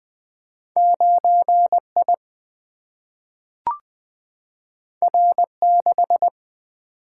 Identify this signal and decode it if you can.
Morse code, heard as a keyed tone, about 20 wpm, 700 Hz. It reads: 9 I R 6
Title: Single-Letter Single-Number 20wpm